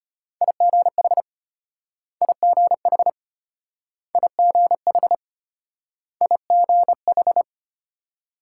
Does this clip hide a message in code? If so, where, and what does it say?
Yes: Morse code from 0.4 to 7.4 s, heard as a keyed tone, about 34 wpm, 700 Hz. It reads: IGH IGH IGH IGH